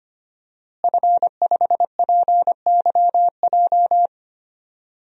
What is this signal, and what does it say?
Morse code, heard as a keyed tone, about 25 wpm, 700 Hz. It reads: F5PYJ
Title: DXCC-100 Call Signs - 5 Characters - Repeated 2x - 25wpm